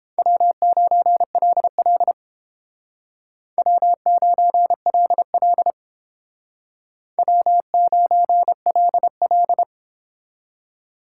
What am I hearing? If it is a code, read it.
Morse code, heard as a keyed tone, about 30 wpm, 700 Hz. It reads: W9LL W9LL W9LL